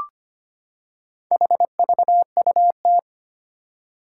Morse code, heard as a keyed tone, about 25 wpm, 700 Hz. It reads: HVUT
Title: Four Challenging Letter-Number Combinations 25wpm